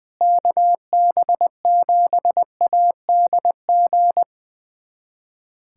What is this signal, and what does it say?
Morse code, heard as a keyed tone, about 20 wpm, 700 Hz. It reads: KB7ADG
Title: USA Callsign DE Callsign - 6 Characters - 20wpm